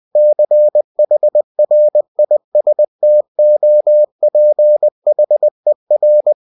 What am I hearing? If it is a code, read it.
Morse code, heard as a keyed tone, about 20 wpm, 600 Hz. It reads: CHRISTOPHER
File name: cw-books / winnie-the-pooh / winnie-20wpm-page-0025